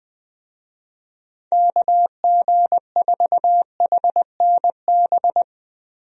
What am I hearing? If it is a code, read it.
Morse code, heard as a keyed tone, about 19 wpm, 700 Hz. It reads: KG4HNB
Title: USA Call Signs - 6 Characters - Repeated 3x - 20wpm